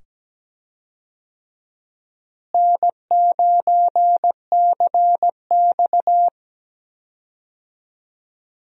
Morse code, heard as a keyed tone, about 17 wpm, 700 Hz. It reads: N9CX
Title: USA Call Signs - 4 Characters - Repeated 2x - 17wpm